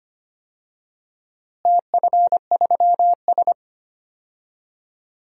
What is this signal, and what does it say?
Morse code, heard as a keyed tone, about 25 wpm, 700 Hz. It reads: TF3S
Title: DXCC-100 Call Signs - 4 Characters - Repeated 2x - 25wpm